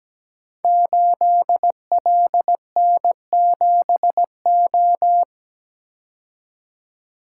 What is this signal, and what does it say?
Morse code, heard as a keyed tone, about 17 wpm, 700 Hz. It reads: 8LN7O
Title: Code Groups - 5 Letter:Numbers - 17wpm